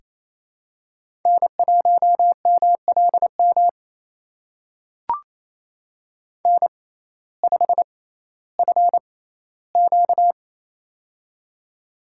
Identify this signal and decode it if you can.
Morse code, heard as a keyed tone, about 27 wpm, spaced as for 15 wpm, 700 Hz. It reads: N1MLM N5FQ
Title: USA Call Signs - 6x Character Spacing - 28wpm